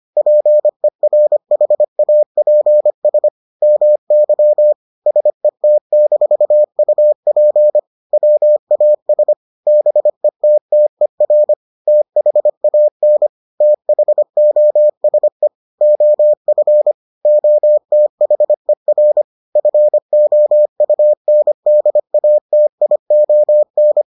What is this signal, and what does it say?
Morse code, heard as a keyed tone, about 25 wpm, 600 Hz. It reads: PERHAPS MY SET-UP WAS BETTER THAN THOSE OF OTHER FOUNDATION